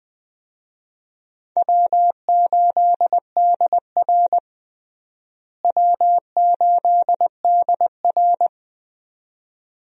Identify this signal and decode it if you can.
Morse code, heard as a keyed tone, about 20 wpm, 700 Hz. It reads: W8DR W8DR
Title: USA Call Signs - 4 Characters - Repeated 2x - 20wpm